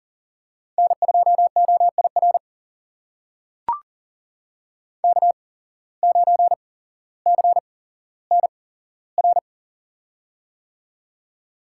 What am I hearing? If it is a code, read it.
Morse code, heard as a keyed tone, about 38 wpm, spaced as for 17 wpm, 700 Hz. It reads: N2OIR K9CNR
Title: USA Call Signs - 8x Character Spacing - 40wpm